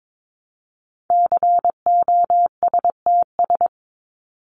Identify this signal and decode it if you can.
Morse code, heard as a keyed tone, about 22 wpm, 700 Hz. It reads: COSTS
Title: ES:S Suffix - Top 100 Words with Base Words 22wpm